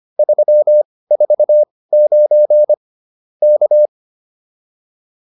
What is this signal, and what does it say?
Morse code, heard as a keyed tone, about 25 wpm, spaced as for 18 wpm, 590 Hz. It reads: 349 K
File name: POTA206 18